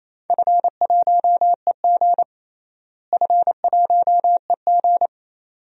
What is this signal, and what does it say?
Morse code, heard as a keyed tone, about 28 wpm, 700 Hz. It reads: F1EG F1EG
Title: DXCC-100 Call Signs - 4 Characters - Repeated 2x - 28wpm